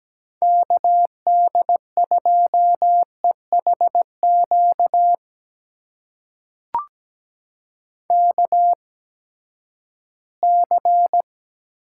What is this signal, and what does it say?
Morse code, heard as a keyed tone, about 17 wpm, spaced as for 12 wpm, 700 Hz. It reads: KD2EHQ KC
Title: USA Call Signs - 6 Characters - 8x Character Spacing - 17wpm